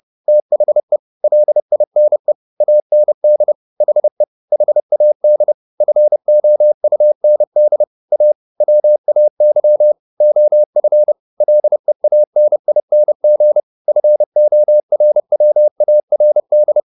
Morse code, heard as a keyed tone, about 30 wpm, 600 Hz. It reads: THE LINE AND HE HAD FOUND A WAY OF LEANING FORWARD
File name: the old man and the sea / the old man and the sea 1 30wpm